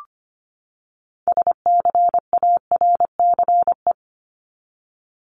Morse code, heard as a keyed tone, about 25 wpm, 700 Hz. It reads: SCARCE
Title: LY Suffix - Top 100 Words with Base Words 25wpm